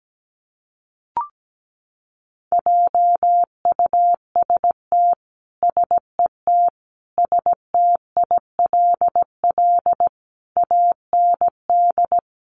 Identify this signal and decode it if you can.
Morse code, heard as a keyed tone, about 17 wpm, 700 Hz. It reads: JUST SET STILL AND
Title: Sentences from Top 300 Words 17wpm